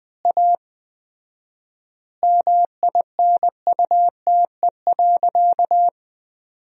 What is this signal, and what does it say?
Morse code, heard as a keyed tone, about 20 wpm, 700 Hz. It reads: A MINUTE.
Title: 4x Word-Spacing - Sentences from Top 500 Words - 20wpm